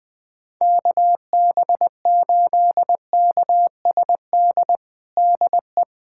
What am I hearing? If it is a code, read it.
Morse code, heard as a keyed tone, about 20 wpm, 700 Hz. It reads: KB8KSD DE
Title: USA Callsign DE Callsign 20wpm